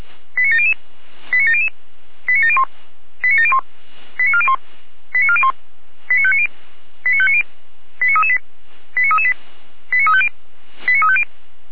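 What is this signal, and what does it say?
Five-tone selective call (ZVEI), heard as quick runs of five tones, tones of about 70 ms. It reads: 89700 89700 89701 89701 89401 89401 89590 89590 89338 89338 89350 89350
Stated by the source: ZVEI five-tone call received by SDR, fire brigade Orlamünde.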